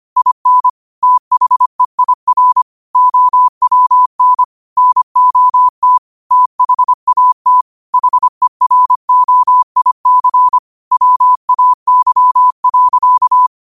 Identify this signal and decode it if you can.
Morse code, heard as a keyed tone, about 25 wpm, 1000 Hz. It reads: IN THEIR OWN NOT THAT HEROIC WAY.
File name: qotdfiles / 25wpm